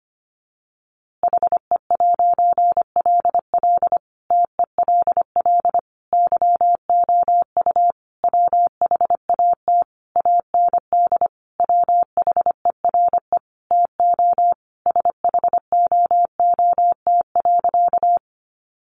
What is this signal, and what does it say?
Morse code, heard as a keyed tone, about 25 wpm, 700 Hz. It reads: HE'LL TELL YOU WHAT AND WHERE TO SHOOT.
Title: A Sound Of Thunder 25wpm